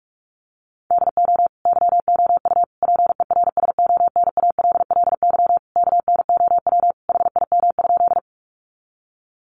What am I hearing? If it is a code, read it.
Morse code, heard as a keyed tone, about 44 wpm, 700 Hz. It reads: DO YOU PERSONALLY KNOW HIM?
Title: Sentences from Top 1400 Words 45wpm